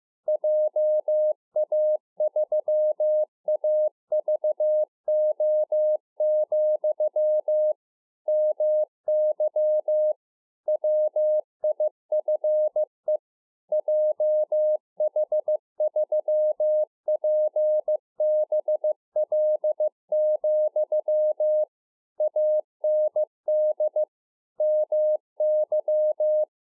Morse code, heard as a keyed tone, about 15 wpm, 610 Hz. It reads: JA3AVO, MY WIFE JH3PBL, AND MY